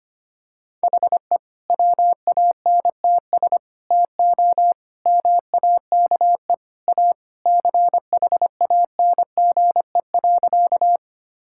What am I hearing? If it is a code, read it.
Morse code, heard as a keyed tone, about 25 wpm, 700 Hz. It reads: HE WANTS TO MAKE A CHANGE.
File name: practice / 25wpm / Sentences with 3-Word Phrases - 25wpm